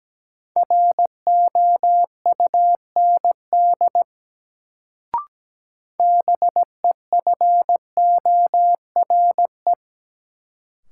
Morse code, heard as a keyed tone, about 17 wpm, 700 Hz. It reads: ROUND BEFORE
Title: Sentences from Top 200 Words - New Word Drill - 17wpm